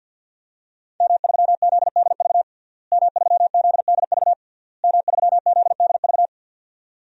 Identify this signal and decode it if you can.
Morse code, heard as a keyed tone, about 49 wpm, 700 Hz. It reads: M3ZDV M3ZDV M3ZDV
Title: DXCC-100 Call Signs - 5 Characters - Repeated 3x - 50wpm